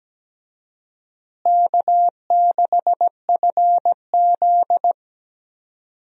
Morse code, heard as a keyed tone, about 17 wpm, 700 Hz. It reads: K6FZ
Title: USA Call Signs - 4 Characters - Repeated 2x - 17wpm